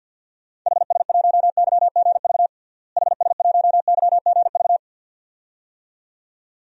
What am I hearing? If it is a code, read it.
Morse code, heard as a keyed tone, about 52 wpm, 700 Hz. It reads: HS1YGV HS1YGV
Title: DXCC-100 Call Signs - 6 Characters - Repeated 2x - 50wpm